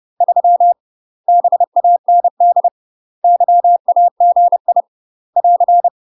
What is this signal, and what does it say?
Morse code, heard as a keyed tone, about 30 wpm, 700 Hz. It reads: 3 BAND YAGI <AR>
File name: UK MASTER QSOs TRAINING / 30WPM / UK MASTER QSOs 0004